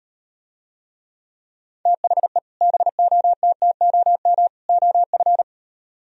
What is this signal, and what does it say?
Morse code, heard as a keyed tone, about 38 wpm, 700 Hz. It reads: THE BOTTOM OF